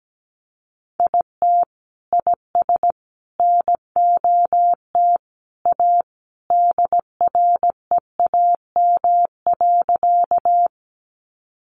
Morse code, heard as a keyed tone, about 17 wpm, 700 Hz. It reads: IT IS NOT A DREAM.
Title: Sentences from Top 1500 Words 17wpm